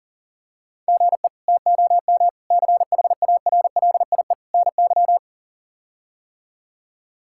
Morse code, heard as a keyed tone, about 40 wpm, 700 Hz. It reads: GE TOM CHARLIE NY
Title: K1USN SST QSO Practice - 40wpm